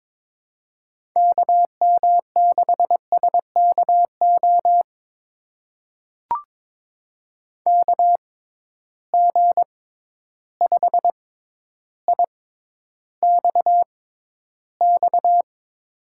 Morse code, heard as a keyed tone, about 22 wpm, spaced as for 11 wpm, 700 Hz. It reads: KM6SKO KG5IXX
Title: USA Call Signs - 6 Characters - 6x Character Spacing - 22wpm